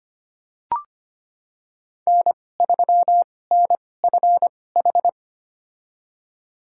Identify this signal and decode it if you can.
Morse code, heard as a keyed tone, about 25 wpm, spaced as for 18 wpm, 700 Hz. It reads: N3NFH
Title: USA Call Signs - 5 Characters - 2x Character Spacing - 25wpm